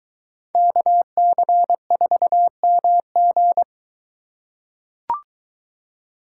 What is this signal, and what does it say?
Morse code, heard as a keyed tone, about 23 wpm, 700 Hz. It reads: KC4MG